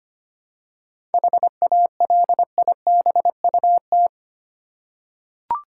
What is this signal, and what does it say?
Morse code, heard as a keyed tone, about 25 wpm, 700 Hz. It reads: HALIBUT
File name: practice / 25wpm / Difficult Words with Hints - 25wpm